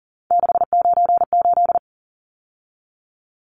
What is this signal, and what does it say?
Morse code, heard as a keyed tone, about 40 wpm, 700 Hz. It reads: 698
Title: Numbers - 3-Digit 40wpm